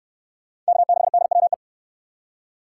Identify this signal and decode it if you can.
Morse code, heard as a keyed tone, about 68 wpm, 700 Hz. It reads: K6DWE